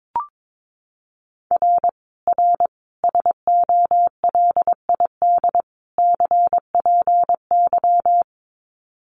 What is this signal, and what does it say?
Morse code, heard as a keyed tone, about 22 wpm, 700 Hz. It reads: R R SOLID CPY
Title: Common QSO Phrases 22wpm